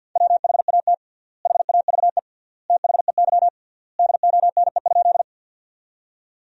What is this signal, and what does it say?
Morse code, heard as a keyed tone, about 50 wpm, 700 Hz. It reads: WHAT HAVE THEY DONE?
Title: 3x Word-Spacing - Sentences from Top 500 Words - 50wpm